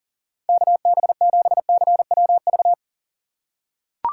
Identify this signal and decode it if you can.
Morse code, heard as a keyed tone, about 37 wpm, 700 Hz. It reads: KB7CWV
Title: USA Call Signs - 6 Characters - 6x Character Spacing - 40wpm